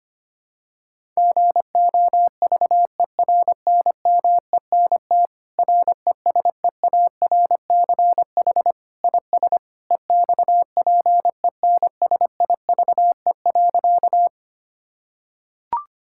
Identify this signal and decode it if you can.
Morse code, heard as a keyed tone, about 25 wpm, 700 Hz. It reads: GOVERNMENT RESEARCH IS EXPENSIVE.
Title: Sentences from Top 1000 Words - Review Part 6 - 25wpm